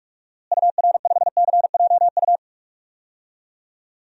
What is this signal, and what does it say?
Morse code, heard as a keyed tone, about 46 wpm, 700 Hz. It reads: UR5CJU
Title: DXCC-100 Call Signs - 6 Characters - Repeated 2x - 45wpm